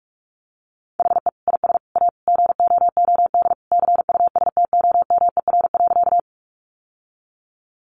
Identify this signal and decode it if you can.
Morse code, heard as a keyed tone, about 44 wpm, 700 Hz. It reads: HE IS A GOOD CUSTOMER.
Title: Sentences from Top 1100 Words 45wpm